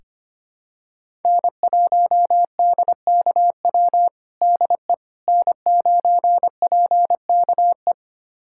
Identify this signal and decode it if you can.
Morse code, heard as a keyed tone, about 25 wpm, 700 Hz. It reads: N1DKW DE N9PKE
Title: USA Callsign DE Callsign - 5 Characters - 25wpm